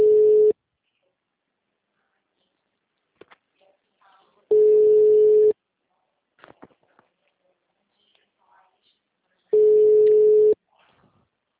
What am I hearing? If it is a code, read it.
Ringback tone (Europe, 425 Hz), with speech.
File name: out- -2014- - - .1564